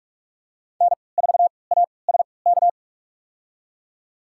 Morse code, heard as a keyed tone, about 42 wpm, spaced as for 25 wpm, 700 Hz. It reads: N4ASK
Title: USA Call Signs - 3x Character Spacing - 45wpm